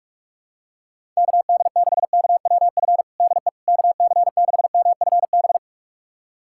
Morse code, heard as a keyed tone, about 46 wpm, 700 Hz. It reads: KD6KWF DE KC6MFB
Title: USA Callsign DE Callsign - 6 Characters - 45wpm